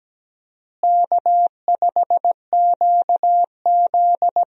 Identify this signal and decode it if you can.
Morse code, heard as a keyed tone, about 17 wpm, 700 Hz. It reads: K5QZ
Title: USA Call Signs - 4 Characters - Repeated 2x - 17wpm